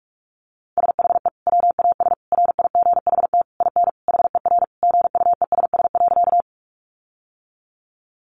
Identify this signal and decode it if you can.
Morse code, heard as a keyed tone, about 44 wpm, 700 Hz. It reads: SHE WAS RIGHT IN HER GUESS.
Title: Sentences from Top 1500 Words 45wpm